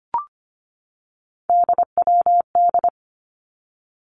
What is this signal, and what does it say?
Morse code, heard as a keyed tone, about 25 wpm, 700 Hz. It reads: DWD
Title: Three Challenging Letter-Number Combinations 25wpm